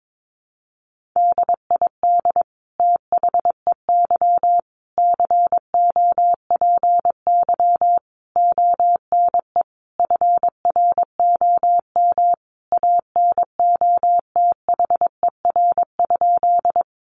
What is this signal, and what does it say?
Morse code, heard as a keyed tone, about 22 wpm, 700 Hz. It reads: DID THEY COPY ONE FROM ANOTHER?
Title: Sentences from Top 1400 Words 22wpm